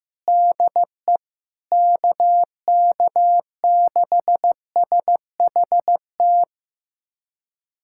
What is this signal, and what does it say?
Morse code, heard as a keyed tone, about 15 wpm, 700 Hz. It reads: DE KK6SHT
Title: USA Callsign DE Callsign 15wpm